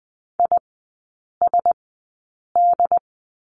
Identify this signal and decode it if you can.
Morse code, heard as a keyed tone, about 20 wpm, spaced as for 8 wpm, 700 Hz. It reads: ISD
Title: 152 Lesson 25e - Letter p - Three Characters 20wpm